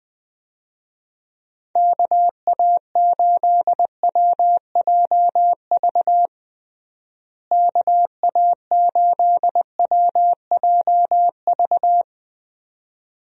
Morse code, heard as a keyed tone, about 20 wpm, 700 Hz. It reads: KA8WJV KA8WJV
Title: USA Call Signs - 6 Characters - Repeated 2x - 20wpm